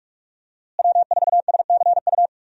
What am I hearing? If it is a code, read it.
Morse code, heard as a keyed tone, about 44 wpm, 700 Hz. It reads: W4SCU
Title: USA Call Signs - 5 Characters - Repeated 2x - 45wpm